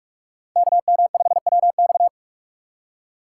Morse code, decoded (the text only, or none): KM5WX